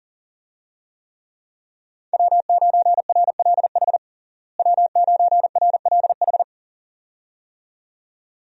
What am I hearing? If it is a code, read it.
Morse code, heard as a keyed tone, about 40 wpm, 700 Hz. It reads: W9RLH W9RLH
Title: USA Call Signs - 5 Characters - Repeated 2x - 40wpm